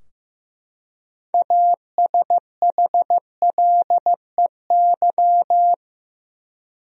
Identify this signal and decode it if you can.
Morse code, heard as a keyed tone, about 15 wpm, 700 Hz. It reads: ASHLEY
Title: USA Common Names - 15wpm